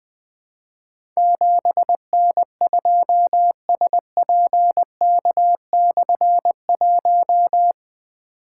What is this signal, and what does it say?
Morse code, heard as a keyed tone, about 20 wpm, 700 Hz. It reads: 7N2SPK/1
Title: Difficult Call Signs 20wpm